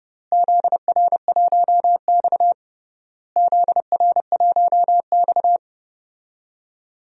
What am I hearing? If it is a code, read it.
Morse code, heard as a keyed tone, about 32 wpm, 700 Hz. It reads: ZR1X ZR1X
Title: DXCC-100 Call Signs - 4 Characters - Repeated 3x - 30wpm